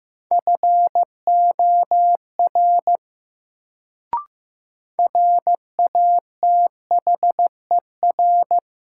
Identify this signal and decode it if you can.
Morse code, heard as a keyed tone, about 15 wpm, 700 Hz. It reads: FOR RATHER
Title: Sets of 2 Words - New Word Drill - 15wpm